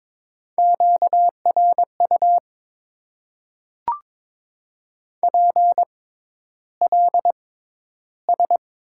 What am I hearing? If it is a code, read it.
Morse code, heard as a keyed tone, about 22 wpm, spaced as for 11 wpm, 700 Hz. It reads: QRU PLS
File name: Top 90 QSO Elements - 6x Character Spacing - 22wpm